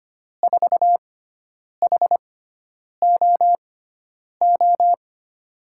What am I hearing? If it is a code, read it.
Morse code, heard as a keyed tone, about 25 wpm, spaced as for 9 wpm, 700 Hz. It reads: 4HOO
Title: USA Call Signs - 6 Characters - 6x Character Spacing - 25wpm